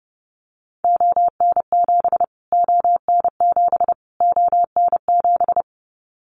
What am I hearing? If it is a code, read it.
Morse code, heard as a keyed tone, about 30 wpm, 700 Hz. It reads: ON7 ON7 ON7
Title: Callsign Prefixes for Top 25 DXCC - Repeated 3x - 30wpm